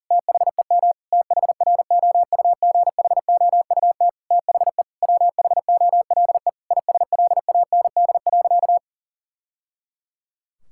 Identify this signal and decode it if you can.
Morse code, heard as a keyed tone, about 40 wpm, 700 Hz. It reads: THEM THROUGHOUT THE WHOLE ISLAND.
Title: Sentences from Top 1000 Words - Review Part 5 - 40wpm